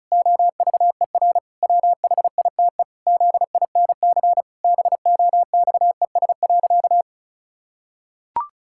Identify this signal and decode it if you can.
Morse code, heard as a keyed tone, about 36 wpm, 700 Hz. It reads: OVER WHITE ZINC BOXES.
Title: Pangram Challenge v2 - 35wpm